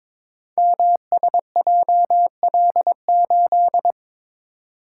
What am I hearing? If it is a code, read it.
Morse code, heard as a keyed tone, about 22 wpm, 700 Hz. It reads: MSJL8
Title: Code Groups - 5 Letter:Numbers - 22wpm